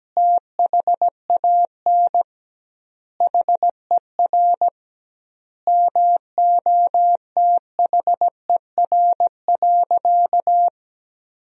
Morse code, heard as a keyed tone, about 17 wpm, 700 Hz. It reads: THAN HER MOTHER.
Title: 2x Word-Spacing - Sentences from Top 400 Words - 17wpm